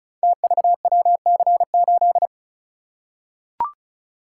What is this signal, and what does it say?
Morse code, heard as a keyed tone, about 36 wpm, 700 Hz. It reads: TVWC8
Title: Code Groups - 5 Letter:Numbers - 35wpm